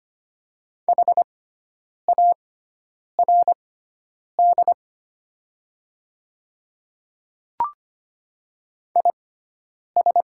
Morse code, heard as a keyed tone, about 25 wpm, spaced as for 9 wpm, 700 Hz. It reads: HARD IS
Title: 6x Character Spacing - Top 500 Words - 25wpm